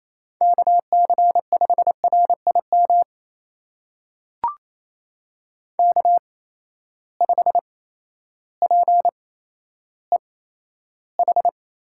Morse code, heard as a keyed tone, about 27 wpm, spaced as for 13 wpm, 700 Hz. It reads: KC5RIM K5PEH
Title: USA Call Signs - 8x Character Spacing - 28wpm